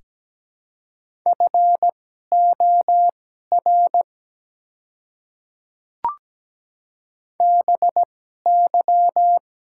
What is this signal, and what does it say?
Morse code, heard as a keyed tone, about 17 wpm, spaced as for 12 wpm, 700 Hz. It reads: FOR BY
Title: 2x Character Spacing - Top 100 Words - 17wpm